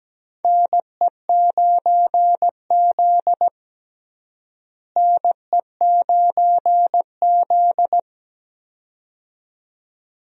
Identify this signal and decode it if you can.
Morse code, heard as a keyed tone, about 17 wpm, 700 Hz. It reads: NE9Z NE9Z
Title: USA Call Signs - 4 Characters - Repeated 2x - 17wpm